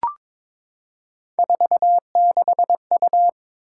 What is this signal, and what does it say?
Morse code, heard as a keyed tone, about 23 wpm, 700 Hz. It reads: 46U
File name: practice / 22wpm / Three Letter-Number Combinations - 22wpm